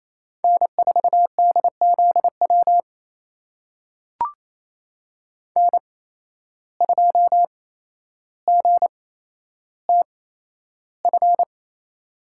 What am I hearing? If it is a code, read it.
Morse code, heard as a keyed tone, about 28 wpm, spaced as for 12 wpm, 700 Hz. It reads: N4DZW N2GTF
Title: USA Call Signs - 5 Characters - 8x Character Spacing - 28wpm